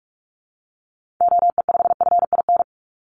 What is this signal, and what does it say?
Morse code, heard as a keyed tone, about 46 wpm, 700 Hz. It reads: OE5FIN
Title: Call Signs - 45wpm